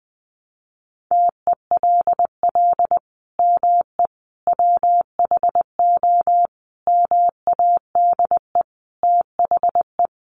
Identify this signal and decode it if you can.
Morse code, heard as a keyed tone, about 19 wpm, 700 Hz. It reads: TELL ME WHO MADE THE
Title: Sentences from Top 200 Words 20wpm